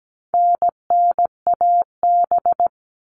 Morse code, heard as a keyed tone, about 17 wpm, 700 Hz. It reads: NNAB
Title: All Cut Numbers - 4-Digit 17wpm